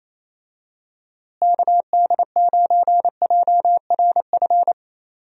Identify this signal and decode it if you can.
Morse code, heard as a keyed tone, about 28 wpm, 700 Hz. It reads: KD9JRF
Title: USA Call Signs - 6 Characters - Repeated 3x - 28wpm